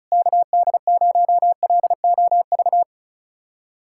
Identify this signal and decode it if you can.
Morse code, heard as a keyed tone, about 34 wpm, 700 Hz. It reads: KD0LOV
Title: USA Callsign DE Callsign 35wpm